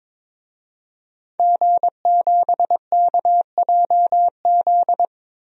Morse code, heard as a keyed tone, about 22 wpm, 700 Hz. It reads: G7KJZ
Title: DXCC-100 Call Signs - 5 Characters - 22wpm